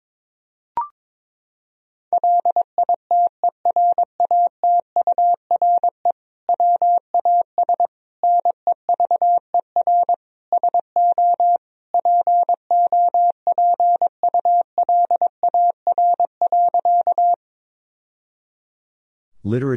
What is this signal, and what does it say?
Morse code, heard as a keyed tone, about 22 wpm, 700 Hz. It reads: LITERATURE WAS NEVER SO POPULAR.
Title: Sentences from Top 1000 Words - Review Part 3 - 22wpm